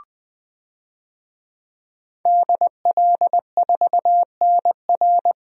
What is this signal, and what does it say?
Morse code, heard as a keyed tone, about 20 wpm, 700 Hz. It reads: DL4NR
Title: DXCC-100 Call Signs - 5 Characters - 20wpm